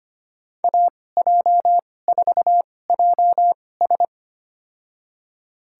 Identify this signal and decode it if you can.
Morse code, heard as a keyed tone, about 25 wpm, spaced as for 18 wpm, 700 Hz. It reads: AJ4JS
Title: USA Call Signs - 5 Characters - 2x Character Spacing - 25wpm